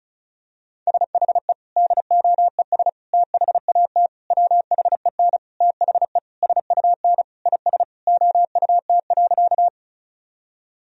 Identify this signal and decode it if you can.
Morse code, heard as a keyed tone, about 36 wpm, 700 Hz. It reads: SHE DOES THAT WHEN THE SUN IS OUT.
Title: Sentences from Top 1000 Words - Review Part 1 - 35wpm